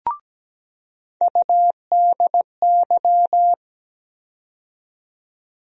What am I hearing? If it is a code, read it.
Morse code, heard as a keyed tone, about 17 wpm, 700 Hz. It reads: UDY
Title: USA Call Sign Postfixes - 3 Letters 17wpm